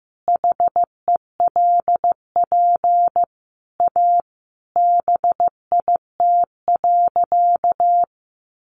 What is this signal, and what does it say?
Morse code, heard as a keyed tone, about 15 wpm, 700 Hz. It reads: HELP A BIT.
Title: Sentences from Top 900 Words 15wpm